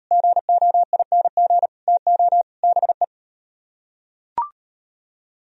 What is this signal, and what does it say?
Morse code, heard as a keyed tone, about 38 wpm, 700 Hz. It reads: GOING TO BE